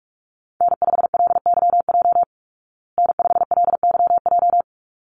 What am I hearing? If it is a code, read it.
Morse code, heard as a keyed tone, about 44 wpm, 700 Hz. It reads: N5LYJ N5LYJ
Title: Call Signs Repeated 3x - 45wpm